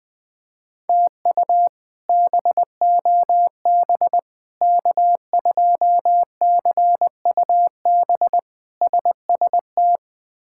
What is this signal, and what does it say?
Morse code, heard as a keyed tone, about 21 wpm, 700 Hz. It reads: TU BOB K2CUB SST
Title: K1USN SST QSO Practice - 20wpm